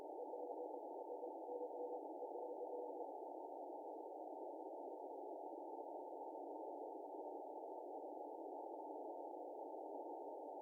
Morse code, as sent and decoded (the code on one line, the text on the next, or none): none